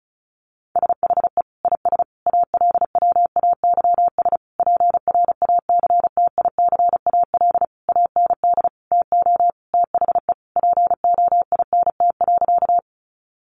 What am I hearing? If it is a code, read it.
Morse code, heard as a keyed tone, about 34 wpm, 700 Hz. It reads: SHE IS ALWAYS PRACTICAL AND TO THE POINT.
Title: Sentences from Top 1400 Words 35wpm